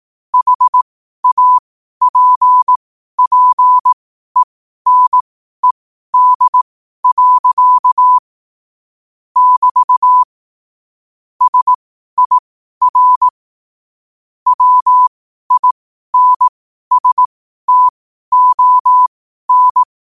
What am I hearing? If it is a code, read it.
Morse code, heard as a keyed tone, about 18 wpm, spaced as for 12 wpm, 1000 Hz. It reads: HAPPENED. <BT> SIR WINSTON
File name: qotdfiles / 12wpm